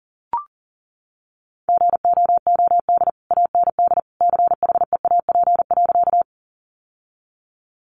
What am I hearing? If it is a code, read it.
Morse code, heard as a keyed tone, about 40 wpm, 700 Hz. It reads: GOOD AND CHEAP.
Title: Sentences from Top 1200 Words 40wpm